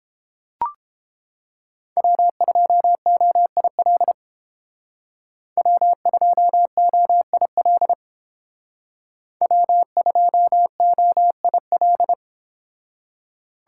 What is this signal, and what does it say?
Morse code, heard as a keyed tone, about 30 wpm, 700 Hz. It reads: W2OIL W2OIL W2OIL